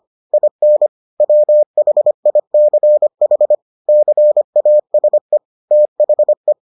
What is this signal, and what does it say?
Morse code, heard as a keyed tone, about 25 wpm, 600 Hz. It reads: IN WHICH CASE THE